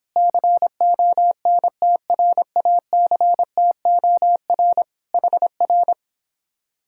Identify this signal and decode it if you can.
Morse code, heard as a keyed tone, about 26 wpm, 700 Hz. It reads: CONTRACTOR HR
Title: Speed-Racing Common QSO Phrases - 22wpm